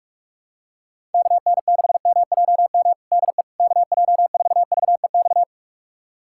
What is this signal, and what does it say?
Morse code, heard as a keyed tone, about 47 wpm, 700 Hz. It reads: KN6MJM DE KJ4VEX
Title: USA Callsign DE Callsign - 6 Characters - 45wpm